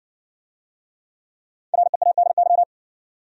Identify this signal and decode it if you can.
Morse code, heard as a keyed tone, about 60 wpm, 700 Hz. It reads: READY